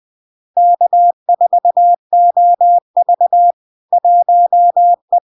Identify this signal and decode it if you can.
Morse code, heard as a keyed tone, about 20 wpm, 700 Hz. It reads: K4OV 1E
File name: LARK-FD-2022-Log-in-CW-20wpm-700-Hz